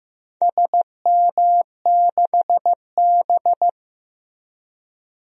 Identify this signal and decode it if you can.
Morse code, heard as a keyed tone, about 15 wpm, 700 Hz. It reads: SM6B
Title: DXCC-100 Call Signs - 4 Characters - 15wpm